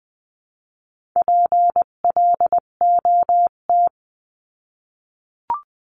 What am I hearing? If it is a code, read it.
Morse code, heard as a keyed tone, about 20 wpm, 700 Hz. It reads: PLOT
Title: 258 Lesson 42c - Words 16-20wpm